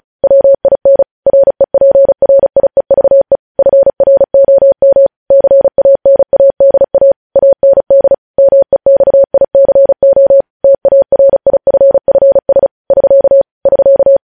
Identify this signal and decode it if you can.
Morse code, heard as a keyed tone, about 36 wpm, 550 Hz. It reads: WIN REPRIEVE FROM CANADA AND MEXICO TARIFFS <SK> <SK>
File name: podcast / News headlines-1x-35x35